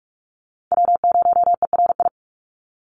Morse code, heard as a keyed tone, about 46 wpm, 690 Hz. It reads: W0ERI
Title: NA SOTA Chaser Callsigns 45wpm